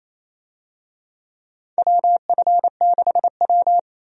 Morse code, heard as a keyed tone, about 27 wpm, 700 Hz. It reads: WF6W
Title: USA Call Signs - 4 Characters - 28wpm